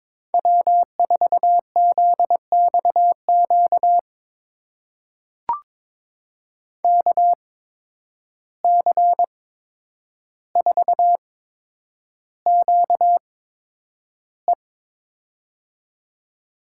Morse code, heard as a keyed tone, about 22 wpm, spaced as for 9 wpm, 700 Hz. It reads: W4ZXQ KC4QE
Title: USA Call Signs - 5 Characters - 8x Character Spacing - 22wpm